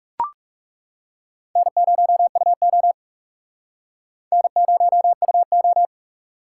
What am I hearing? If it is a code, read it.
Morse code, heard as a keyed tone, about 38 wpm, 700 Hz. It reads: N0UO N0UO